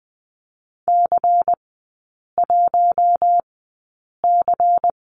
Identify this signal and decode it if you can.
Morse code, heard as a keyed tone, about 20 wpm, spaced as for 8 wpm, 690 Hz. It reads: C1C
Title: 093 Lesson 16e - Letter c - Three Characters 20wpm